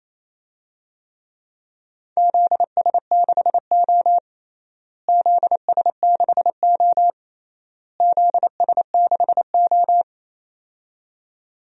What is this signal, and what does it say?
Morse code, heard as a keyed tone, about 28 wpm, 700 Hz. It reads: ZS6O ZS6O ZS6O
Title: DXCC-100 Call Signs - 4 Characters - Repeated 3x - 28wpm